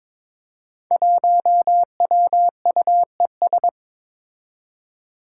Morse code, heard as a keyed tone, about 21 wpm, 700 Hz. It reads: 1WUES
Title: Five Challenging Letter-Number Combinations 22wpm